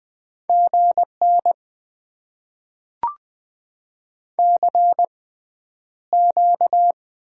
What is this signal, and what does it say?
Morse code, heard as a keyed tone, about 20 wpm, spaced as for 10 wpm, 700 Hz. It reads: GN CQ
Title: Top 90 QSO Elements - 6x Character Spacing - 20wpm